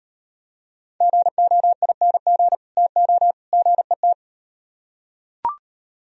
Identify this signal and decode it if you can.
Morse code, heard as a keyed tone, about 38 wpm, 700 Hz. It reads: GOING TO GET